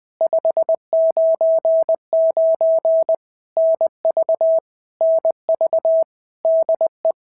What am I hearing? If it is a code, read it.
Morse code, heard as a keyed tone, about 20 wpm, 650 Hz. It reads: 599 NV NV DE
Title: POTA QSO Practice - 20wpm